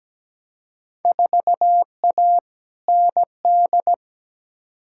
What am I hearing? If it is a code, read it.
Morse code, heard as a keyed tone, about 17 wpm, 700 Hz. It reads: 4A ND
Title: Field Day Exchanges Encore - 17wpm